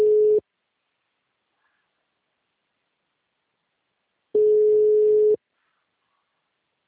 Ringback tone (Europe, 425 Hz).